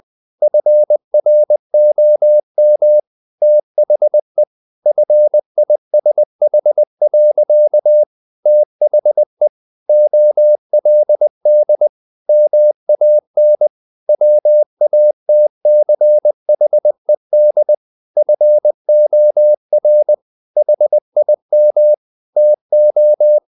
Morse code, heard as a keyed tone, about 20 wpm, 600 Hz. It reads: FROM THE FISH. THE OLD MAN WATCHED FOR HIM TO
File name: the old man and the sea part 2 20wpm